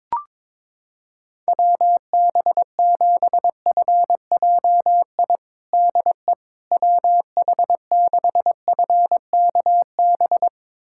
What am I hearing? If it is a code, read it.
Morse code, heard as a keyed tone, about 23 wpm, 700 Hz. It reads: WB7FJI DE WH6FKB
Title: USA Callsign DE Callsign - 6 Characters - 22wpm